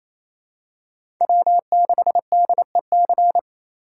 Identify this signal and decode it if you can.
Morse code, heard as a keyed tone, about 27 wpm, 700 Hz. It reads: W6DEC
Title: USA Call Signs - 5 Characters - 8x Character Spacing - 28wpm